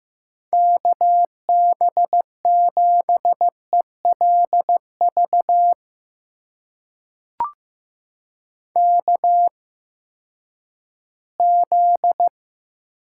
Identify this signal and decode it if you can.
Morse code, heard as a keyed tone, about 15 wpm, spaced as for 10 wpm, 700 Hz. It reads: KB7ELV KZ
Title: USA Call Signs - 8x Character Spacing - 15wpm